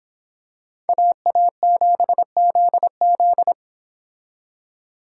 Morse code, heard as a keyed tone, about 25 wpm, 700 Hz. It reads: AA7ZZ